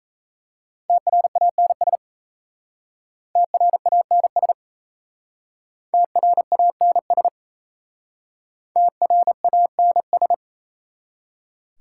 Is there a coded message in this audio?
Yes: Morse code from 0.9 to 10.3 s, heard as a keyed tone, about 37 wpm, 690 Hz. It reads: TRANS TRANS TRANS TRANS